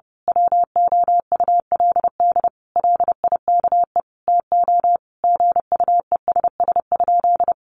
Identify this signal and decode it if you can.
Morse code, heard as a keyed tone, about 30 wpm, 700 Hz. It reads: WOULD LIKE TO GUESS?
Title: Sentences from Top 1500 Words 30wpm